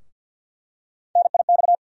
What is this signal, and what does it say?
Morse code, heard as a keyed tone, about 47 wpm, 700 Hz. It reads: NIX
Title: USA Call Sign Postfixes - 3 Letters 50wpm